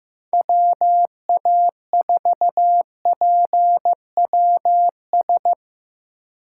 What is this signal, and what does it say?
Morse code, heard as a keyed tone, about 16 wpm, 700 Hz. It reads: WA4PWS
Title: Speed Racing - USA 2x3 Call Signs 15wpm